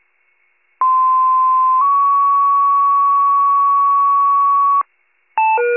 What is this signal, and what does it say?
Two-tone page, heard as a one-second tone, then a three-second tone, about 1040 then 1120 Hz. It is heard from 0.8 to 4.8 s.
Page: County Wide Paging Archive